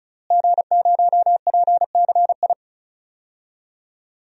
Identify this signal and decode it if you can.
Morse code, heard as a keyed tone, about 36 wpm, 700 Hz. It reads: G0PCI
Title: DXCC-100 Call Signs - 5 Characters - Repeated 3x - 35wpm